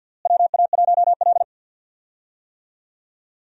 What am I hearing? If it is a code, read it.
Morse code, heard as a keyed tone, about 46 wpm, 700 Hz. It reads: WA1L
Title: USA Call Signs - 4 Characters - Repeated 2x - 50wpm